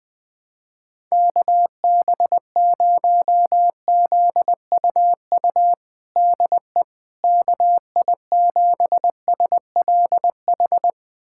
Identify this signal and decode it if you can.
Morse code, heard as a keyed tone, about 20 wpm, 700 Hz. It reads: KB0ZUU DE KI7SLH
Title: USA Callsign DE Callsign - 6 Characters - 20wpm